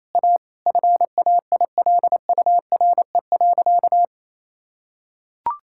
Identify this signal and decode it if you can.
Morse code, heard as a keyed tone, about 27 wpm, 700 Hz. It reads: A FAILURE.
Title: Sentences from Top 1000 Words - Review Part 6 - 28wpm